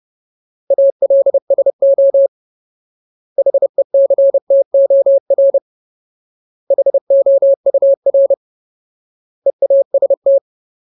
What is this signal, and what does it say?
Morse code, heard as a keyed tone, about 30 wpm, 550 Hz. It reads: ALSO HECTOR HOUR EAST
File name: Ultimate Word List / 30WPM Ultimate